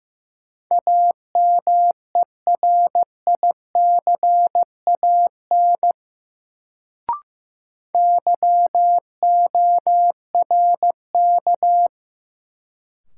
Morse code, heard as a keyed tone, about 15 wpm, 700 Hz. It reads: AMERICAN YORK
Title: Sets of 2 Words - New Word Drill - 15wpm